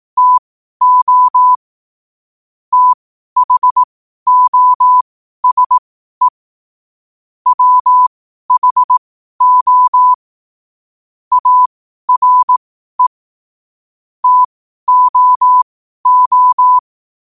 Morse code, heard as a keyed tone, about 18 wpm, spaced as for 12 wpm, 1000 Hz. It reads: TO THOSE WHO ARE TOO
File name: qotdfiles / 12wpm